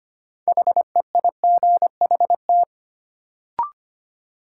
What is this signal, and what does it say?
Morse code, heard as a keyed tone, about 25 wpm, 700 Hz. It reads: HEIGHT
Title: Sentences from Top 1600 Words - New Word Drill Alt - 25wpm